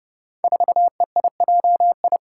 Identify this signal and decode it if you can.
Morse code, heard as a keyed tone, about 30 wpm, 700 Hz. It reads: 4EIJI
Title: Five Challenging Letter-Number Combinations 30wpm